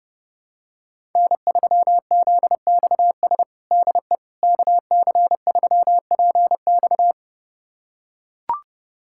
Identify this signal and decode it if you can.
Morse code, heard as a keyed tone, about 32 wpm, 700 Hz. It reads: N3ZXS DE KC3PX
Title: USA Callsign DE Callsign - 5 Characters - 30wpm